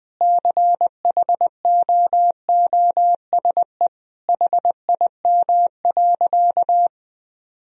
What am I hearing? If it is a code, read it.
Morse code, heard as a keyed tone, about 19 wpm, 700 Hz. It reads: CHOOSE HIM.
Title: Sentences from Top 1000 Words - Review Part 4 - 20wpm